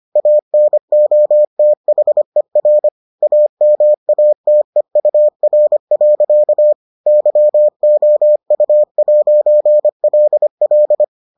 Morse code, heard as a keyed tone, about 25 wpm, 600 Hz. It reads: ANOTHER AMATEUR. YOU'LL